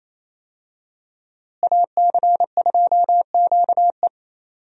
Morse code, heard as a keyed tone, about 28 wpm, 700 Hz. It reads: AC2QE
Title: USA Call Signs - 5 Characters - Repeated 2x - 28wpm